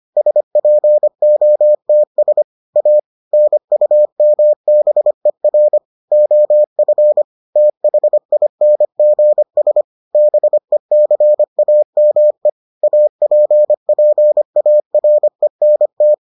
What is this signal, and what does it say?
Morse code, heard as a keyed tone, about 25 wpm, 600 Hz. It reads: SPOTS A NUMBER OF THINGS BECAME APPARENT